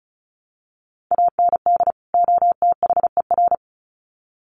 Morse code, heard as a keyed tone, about 34 wpm, 700 Hz. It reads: AND OTHER
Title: Sets of 2 Words 35wpm